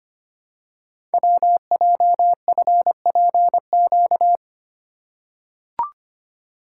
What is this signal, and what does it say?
Morse code, heard as a keyed tone, about 25 wpm, 700 Hz. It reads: WJFPQ
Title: Code Groups - 5 Letters - 25wpm